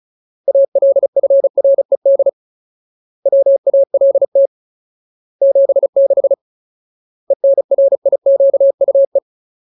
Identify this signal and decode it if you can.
Morse code, heard as a keyed tone, about 36 wpm, 550 Hz. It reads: ALFRED WALT 76 ENRIQUE
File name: Ultimate Word List / 35WPM Ultimate